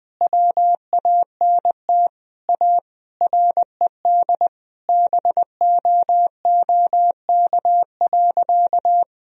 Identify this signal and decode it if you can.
Morse code, heard as a keyed tone, about 21 wpm, 700 Hz. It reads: WANT A RED BOOK.
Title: Sentences from Top 1000 Words - Review Part 2 - 20wpm